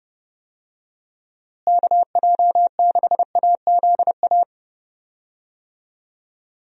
Morse code, heard as a keyed tone, about 30 wpm, 700 Hz. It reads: KJ6AZA
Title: USA Call Signs - 6 Characters - Repeated 3x - 30wpm